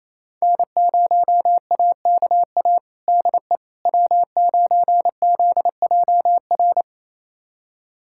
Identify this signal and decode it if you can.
Morse code, heard as a keyed tone, about 28 wpm, 700 Hz. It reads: N0AKA DE W9ZJR
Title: USA Callsign DE Callsign - 5 Characters - 28wpm